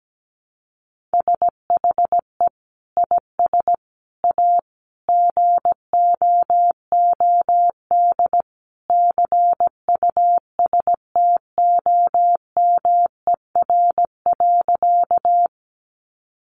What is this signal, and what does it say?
Morse code, heard as a keyed tone, about 17 wpm, 700 Hz. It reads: SHE IS A GOOD CUSTOMER.
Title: Sentences from Top 1100 Words 17wpm